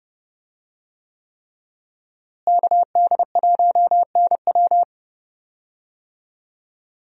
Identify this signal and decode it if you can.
Morse code, heard as a keyed tone, about 30 wpm, 700 Hz. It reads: KD1NW
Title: USA Call Signs - 5 Characters - Repeated 3x - 30wpm